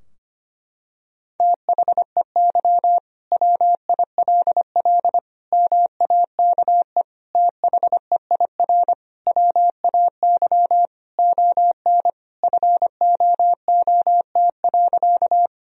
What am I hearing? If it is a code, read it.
Morse code, heard as a keyed tone, about 25 wpm, 700 Hz. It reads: THEY WILL MAKE THEIR WAY ON FOOT.
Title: Sentences from Top 1000 Words - Review Part 3 - 25wpm